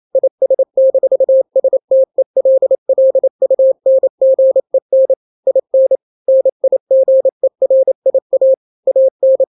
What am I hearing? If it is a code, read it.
Morse code, heard as a keyed tone, about 28 wpm, 530 Hz. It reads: IS-STELLUNGEN IN NIGERIA AN
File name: daily / 27wpm